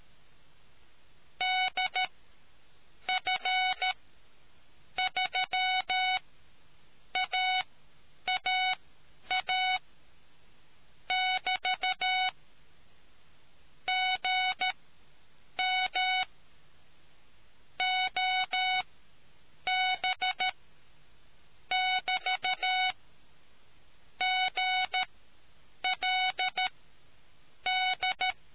Morse code, heard as a keyed tone, about 13 wpm, spaced as for 7 wpm, 740 Hz. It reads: DF3AAA <BT> GM OB <BT> GLD